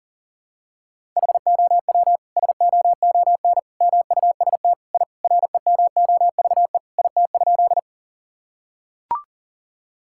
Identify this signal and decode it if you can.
Morse code, heard as a keyed tone, about 40 wpm, 700 Hz. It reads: HOW SOON MUST I REMOVE IT?
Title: Sentences from Top 1000 Words - Review Part 5 - 40wpm